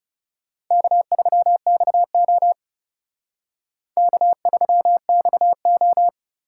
Morse code, heard as a keyed tone, about 39 wpm, 700 Hz. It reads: K3XO K3XO